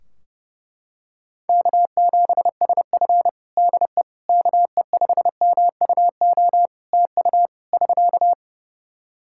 Morse code, heard as a keyed tone, about 30 wpm, 700 Hz. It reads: K7SF DE KE5MUO TU <SK>
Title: K1USN SST QSO Practice 30wpm